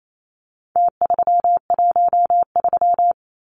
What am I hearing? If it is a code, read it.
Morse code, heard as a keyed tone, about 28 wpm, 700 Hz. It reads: T313
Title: Numbers - Contest Style - 4-Digit 28wpm